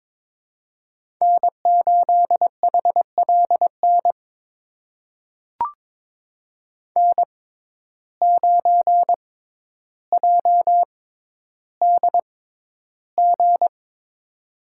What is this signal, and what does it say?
Morse code, heard as a keyed tone, about 22 wpm, spaced as for 11 wpm, 700 Hz. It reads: N8HLN N9JDG
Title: USA Call Signs - 5 Characters - 6x Character Spacing - 22wpm